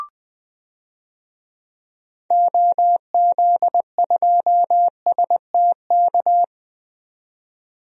Morse code, heard as a keyed tone, about 20 wpm, 700 Hz. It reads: OZ2STK
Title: DXCC-100 Call Signs - 6 Characters - 20wpm